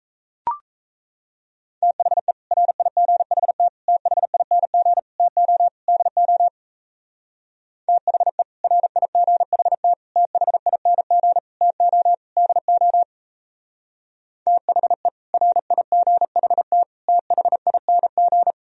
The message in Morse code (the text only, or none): THE RIGHT THING TO DO THE RIGHT THING TO DO THE RIGHT THING